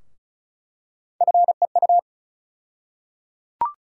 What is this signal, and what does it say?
Morse code, heard as a keyed tone, about 34 wpm, 700 Hz. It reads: FEU
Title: USA Call Sign Postfixes - 3 Letters 35wpm